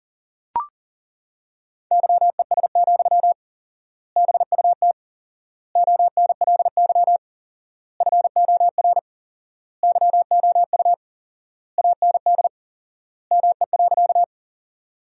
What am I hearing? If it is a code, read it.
Morse code, heard as a keyed tone, about 40 wpm, 700 Hz. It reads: YES, BUT ONLY FOR YOU AND ME.
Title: 4x Word-Spacing - Sentences from Top 500 Words - 40wpm